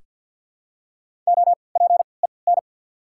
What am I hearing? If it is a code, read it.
Morse code, heard as a keyed tone, about 47 wpm, 690 Hz. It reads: O P E N
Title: 3x Character Spacing - Top 300 Words - 50wpm